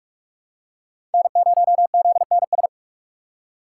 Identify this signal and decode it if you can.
Morse code, heard as a keyed tone, about 46 wpm, 700 Hz. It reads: N0ZNS
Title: USA Call Signs - 5 Characters - 4x Character Spacing - 45wpm